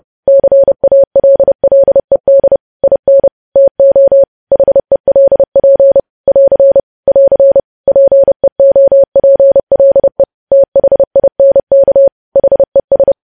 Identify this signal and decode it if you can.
Morse code, heard as a keyed tone, about 30 wpm, 550 Hz. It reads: CALLED IN TO HELP <AR> <AR> PEOPLE THINK HES
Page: news headlines in morse code at 30 wpm